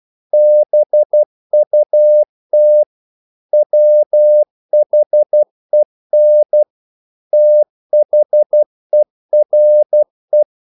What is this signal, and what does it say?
Morse code, heard as a keyed tone, about 12 wpm, 600 Hz. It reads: BUT WHEN THERE
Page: English quotations in CW (telegraphy - morse)